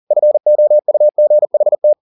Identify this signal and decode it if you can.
Morse code, heard as a keyed tone, about 40 wpm, 600 Hz. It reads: FOUGHT
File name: the old man and the sea 2 40wpm